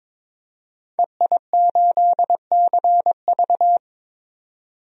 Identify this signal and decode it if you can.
Morse code, heard as a keyed tone, about 22 wpm, 700 Hz. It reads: EI8CV
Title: DXCC-100 Call Signs - 5 Characters - 22wpm